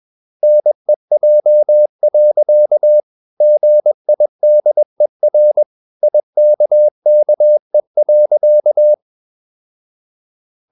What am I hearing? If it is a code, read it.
Morse code, heard as a keyed tone, about 21 wpm, 600 Hz. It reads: NEJ. GIDER IKKE.
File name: Ferie QRS 21wpm